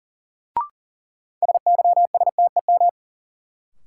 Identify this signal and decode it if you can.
Morse code, heard as a keyed tone, about 40 wpm, 700 Hz. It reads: SYSTEM
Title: Sentences from Top 1000 Words - Review Part 3 - New Word Drill - 40wpm